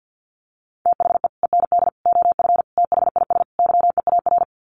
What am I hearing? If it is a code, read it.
Morse code, heard as a keyed tone, about 48 wpm, 700 Hz. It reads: THE END OF THIS YEAR
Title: Sets of 5 Words Encore 50wpm